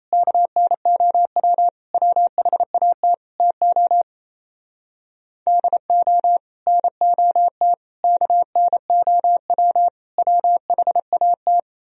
Morse code, heard as a keyed tone, about 35 wpm, 700 Hz. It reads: KNOW WHAT TO DO NOT KNOW WHAT
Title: Speed-Racing - Sets of 5 Words 28wpm